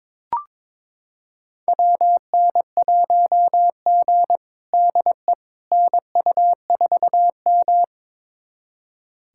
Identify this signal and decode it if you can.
Morse code, heard as a keyed tone, about 22 wpm, 700 Hz. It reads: WN1G DE NU4M
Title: USA Callsign DE Callsign - 4 Characters - 22wpm